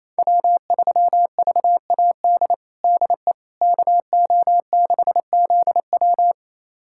Morse code, heard as a keyed tone, about 28 wpm, 700 Hz. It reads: W3VAD DE KO6ZW
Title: USA Callsign DE Callsign - 5 Characters - 28wpm